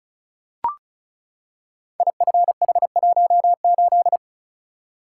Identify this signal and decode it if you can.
Morse code, heard as a keyed tone, about 36 wpm, 700 Hz. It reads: IFH18
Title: Code Groups - 5 Letter:Numbers - 35wpm